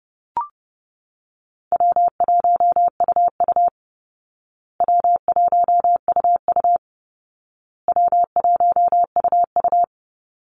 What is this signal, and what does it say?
Morse code, heard as a keyed tone, about 30 wpm, 700 Hz. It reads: W1UU W1UU W1UU
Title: Easy Call Signs Repeated 3x - 30wpm